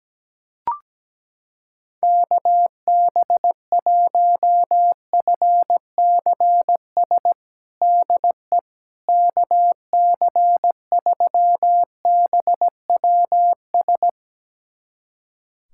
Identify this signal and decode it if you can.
Morse code, heard as a keyed tone, about 17 wpm, 700 Hz. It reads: KB1FCS DE KC3BWS
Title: USA Callsign DE Callsign 17wpm